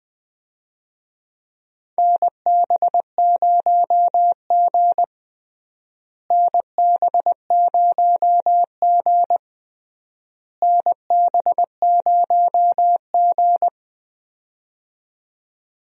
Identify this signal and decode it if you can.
Morse code, heard as a keyed tone, about 20 wpm, 700 Hz. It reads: NB0G NB0G NB0G
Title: USA Call Signs - 4 Characters - Repeated 3x - 20wpm